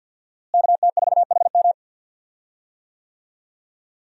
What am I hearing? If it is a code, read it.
Morse code, heard as a keyed tone, about 52 wpm, 700 Hz. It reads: KT4HM
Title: USA Call Signs - 5 Characters - Repeated 2x - 50wpm